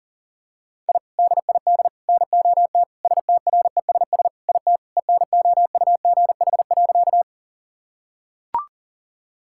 Morse code, heard as a keyed tone, about 37 wpm, 700 Hz. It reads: I DID NOT STRESS IT ENOUGH.
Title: Sentences from Top 1000 Words - Review Part 6 - 40wpm